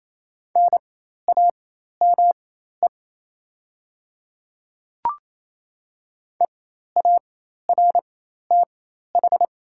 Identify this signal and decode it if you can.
Morse code, heard as a keyed tone, about 27 wpm, spaced as for 13 wpm, 700 Hz. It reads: NAME EARTH
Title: 4x Character Spacing - Top 200 Words - 28wpm